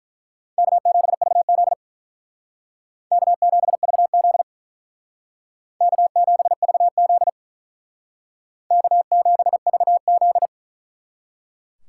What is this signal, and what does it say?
Morse code, heard as a keyed tone, about 42 wpm, 700 Hz. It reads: K7VZ K7VZ K7VZ K7VZ
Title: Speed Racing - USA 1x2 Call Signs 35wpm